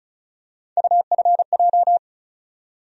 Morse code, heard as a keyed tone, about 36 wpm, 700 Hz. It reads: UFJ
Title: USA Call Sign Postfixes - 3 Letters 35wpm